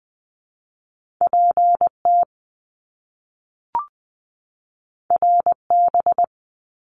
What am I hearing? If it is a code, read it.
Morse code, heard as a keyed tone, about 20 wpm, 700 Hz. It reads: P T R B
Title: Top Two Letter Combinations in Words 20wpm